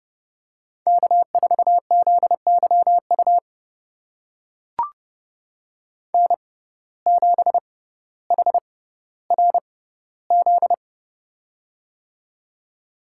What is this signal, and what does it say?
Morse code, heard as a keyed tone, about 31 wpm, spaced as for 15 wpm, 700 Hz. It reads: K4ZYU N7HRZ
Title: USA Call Signs - 5 Characters - 6x Character Spacing - 30wpm